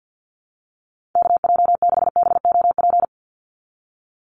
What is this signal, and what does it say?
Morse code, heard as a keyed tone, about 50 wpm, 700 Hz. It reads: KJ6BOP
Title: USA Call Signs 50wpm